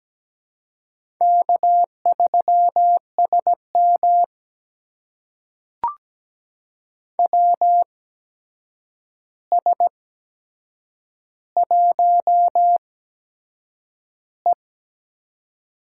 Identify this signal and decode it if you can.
Morse code, heard as a keyed tone, about 17 wpm, spaced as for 7 wpm, 700 Hz. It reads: K3SM WS1E
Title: USA Call Signs - 4 Characters - 8x Character Spacing - 17wpm